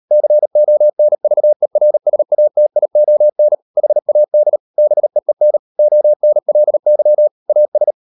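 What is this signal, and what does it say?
Morse code, heard as a keyed tone, about 38 wpm, 600 Hz. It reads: CONVERSATION HAD BEEN ONLY AS
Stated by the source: Morse code practice at 38 WPM, sending Ken Follett's novel Never in English, from CW med Gnister.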